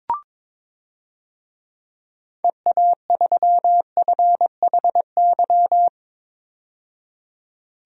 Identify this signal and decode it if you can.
Morse code, heard as a keyed tone, about 22 wpm, 700 Hz. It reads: EA3FHY
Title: DXCC-100 Call Signs - 6 Characters - 22wpm